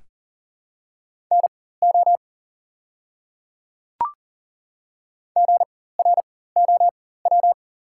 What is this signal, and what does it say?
Morse code, heard as a keyed tone, about 40 wpm, spaced as for 19 wpm, 700 Hz. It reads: NO GROW
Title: 4x Character Spacing - Top 500 Words - 40wpm